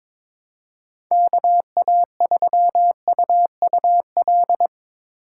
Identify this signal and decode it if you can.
Morse code, heard as a keyed tone, about 22 wpm, 700 Hz. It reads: KA3UUL
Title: USA Call Signs - 6 Characters - 2x Character Spacing - 22wpm